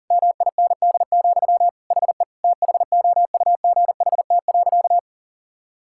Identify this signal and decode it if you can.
Morse code, heard as a keyed tone, about 40 wpm, 700 Hz. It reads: MIND, HE THOUGHT.
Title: Sentences from Top 1000 Words - Review Part 6 - 40wpm